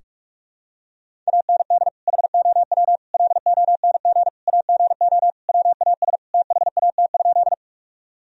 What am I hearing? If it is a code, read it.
Morse code, heard as a keyed tone, about 42 wpm, 700 Hz. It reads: AND HOW LONG AGO WAS THAT?
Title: Sentences from Top 1000 Words - Review Part 3 - 45wpm